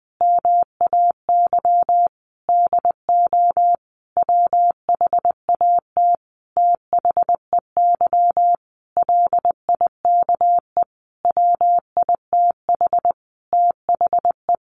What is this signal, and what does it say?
Morse code, heard as a keyed tone, about 20 wpm, 700 Hz. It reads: MAY DO WHAT THEY LIKE WITH THE
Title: Sentences from Top 1200 Words 20wpm